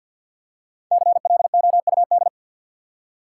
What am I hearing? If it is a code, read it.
Morse code, heard as a keyed tone, about 47 wpm, 700 Hz. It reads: CLOUD
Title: Five Letter Words - 50wpm